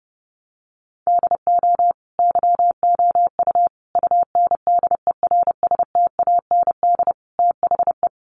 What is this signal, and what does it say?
Morse code, heard as a keyed tone, about 30 wpm, 700 Hz. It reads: DO YOU UNDERSTAND THE
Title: Sentences from Top 1200 Words 30wpm